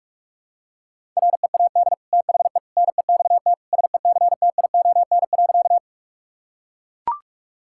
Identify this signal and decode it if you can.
Morse code, heard as a keyed tone, about 41 wpm, 700 Hz. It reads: READ THE NEXT SECTION.
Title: Sentences from Top 1000 Words - Review Part 5 - 45wpm